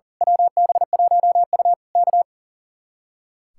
Morse code, heard as a keyed tone, about 35 wpm, 700 Hz. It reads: WB1U K
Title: Speed-Racing Common QSO Phrases - 40wpm